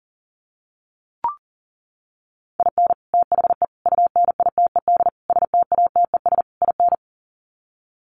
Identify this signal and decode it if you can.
Morse code, heard as a keyed tone, about 40 wpm, 700 Hz. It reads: IN THE UNITED STATES IN
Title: Sets of 5 Words 40wpm